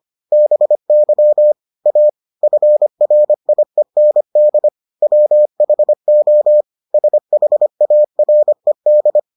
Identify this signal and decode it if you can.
Morse code, heard as a keyed tone, about 25 wpm, 600 Hz. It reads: BY A FRIEND WHO SHARED